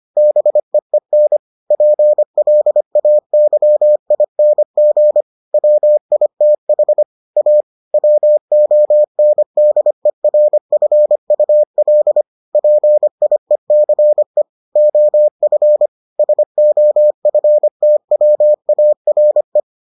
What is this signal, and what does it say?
Morse code, heard as a keyed tone, about 25 wpm, 600 Hz. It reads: BEEN PLAYING WITH A WONDERFUL PIECE OF SOFTWARE